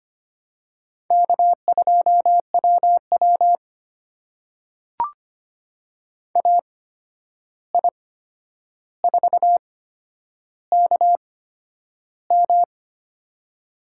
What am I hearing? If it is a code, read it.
Morse code, heard as a keyed tone, about 25 wpm, spaced as for 10 wpm, 700 Hz. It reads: K2WW AI4KM
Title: USA Call Signs - 8x Character Spacing - 25wpm